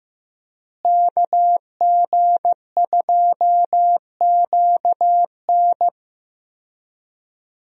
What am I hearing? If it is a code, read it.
Morse code, heard as a keyed tone, about 15 wpm, 700 Hz. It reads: KG2QN
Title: USA Call Signs - 5 Characters - Repeated 2x - 15wpm